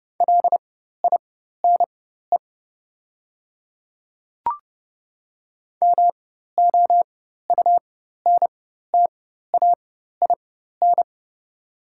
Morse code, heard as a keyed tone, about 29 wpm, spaced as for 14 wpm, 690 Hz. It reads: LINE MOUNTAIN
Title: 4x Character Spacing - Top 500 Words - 30wpm